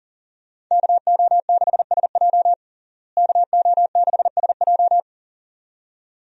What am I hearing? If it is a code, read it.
Morse code, heard as a keyed tone, about 40 wpm, 700 Hz. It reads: KO6SJ KO6SJ
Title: USA Call Signs - 5 Characters - Repeated 2x - 40wpm